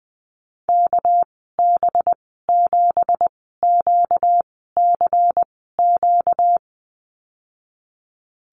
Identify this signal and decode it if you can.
Morse code, heard as a keyed tone, about 20 wpm, spaced as for 14 wpm, 700 Hz. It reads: KB7QCQ
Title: 245 Lesson 39g - Letter z - Call Signs 20wpm